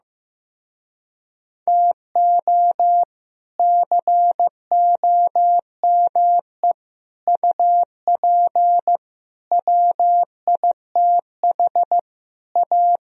Morse code, heard as a keyed tone, about 15 wpm, 700 Hz. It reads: TO COME UP WITH A
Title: Speed-Racing - Sets of 5 Words 15wpm